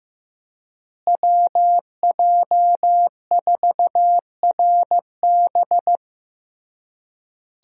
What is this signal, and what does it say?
Morse code, heard as a keyed tone, about 15 wpm, 700 Hz. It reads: WJ4RB
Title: USA Call Signs - 5 Characters - Repeated 3x - 15wpm